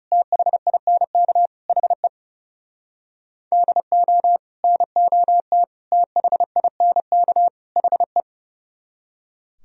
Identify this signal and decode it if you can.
Morse code, heard as a keyed tone, about 40 wpm, 700 Hz. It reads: THINK HE DO NOT THINK HE